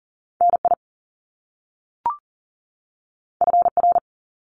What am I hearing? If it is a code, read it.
Morse code, heard as a keyed tone, about 40 wpm, 700 Hz. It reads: NI FR
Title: Top Two Letter Combinations in Words 40wpm